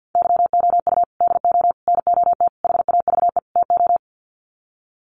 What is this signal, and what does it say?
Morse code, heard as a keyed tone, about 50 wpm, 700 Hz. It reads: YOU DO NOT HAVE TO
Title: Sets of 5 Words 50wpm